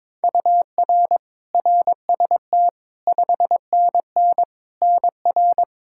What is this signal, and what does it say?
Morse code, heard as a keyed tone, about 22 wpm, 700 Hz. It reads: UR RST 5NN NR
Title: Practice QSO - Exchange 1 - Beginning - 22wpm